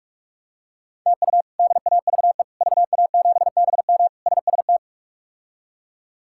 Morse code, heard as a keyed tone, about 41 wpm, 700 Hz. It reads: TU DAVE VA7BM SST
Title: K1USN SST QSO Practice - 45wpm